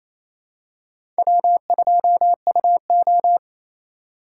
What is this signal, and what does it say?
Morse code, heard as a keyed tone, about 28 wpm, 700 Hz. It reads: W2UO
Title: USA Call Signs - 4 Characters - 6x Character Spacing - 28wpm